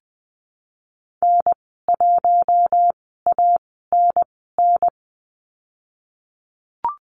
Morse code, heard as a keyed tone, about 20 wpm, spaced as for 15 wpm, 700 Hz. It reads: N1ANN
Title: 069 Lesson 12g - Letter d - Call Signs 20wpm